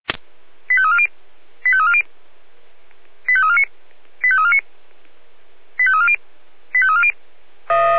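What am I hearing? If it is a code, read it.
Five-tone selective call (ZVEI), heard as quick runs of five tones, tones of about 70 ms. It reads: 85340 85340 85349 85349 85340 85340
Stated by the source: ZVEI five-tone call received by SDR.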